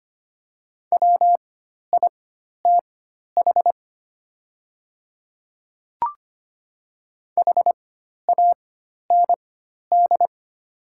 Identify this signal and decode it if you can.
Morse code, heard as a keyed tone, about 25 wpm, spaced as for 12 wpm, 700 Hz. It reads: WITH HAND
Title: 4x Character Spacing - Top 200 Words - 25wpm